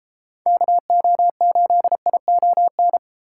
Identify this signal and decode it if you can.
Morse code, heard as a keyed tone, about 29 wpm, 700 Hz. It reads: KO8ION